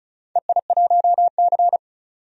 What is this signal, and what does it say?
Morse code, heard as a keyed tone, about 36 wpm, 690 Hz. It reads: EI1C
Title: DXCC-100 Call Signs - 4 Characters - 35wpm